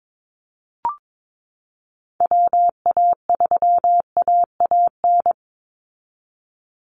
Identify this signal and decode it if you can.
Morse code, heard as a keyed tone, about 22 wpm, 700 Hz. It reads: WA3AAN
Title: Call Signs Repeated 2x - 22wpm